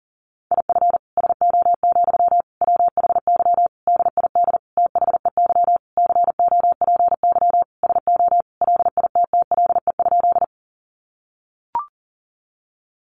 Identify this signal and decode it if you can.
Morse code, heard as a keyed tone, about 40 wpm, 700 Hz. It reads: IF SO, WHY DID THEY COPY SO LITTLE?
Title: Sentences from Top 1400 Words 40wpm